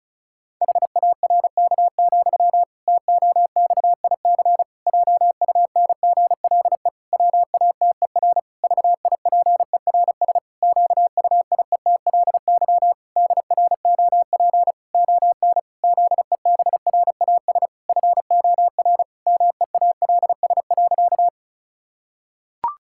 Morse code, heard as a keyed tone, about 36 wpm, 700 Hz. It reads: HARK, TOXIC JUNGLE WATER VIPERS QUIETLY DROP ON ZEBRAS FOR MEALS.
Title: Pangram Challenge v2 - 35wpm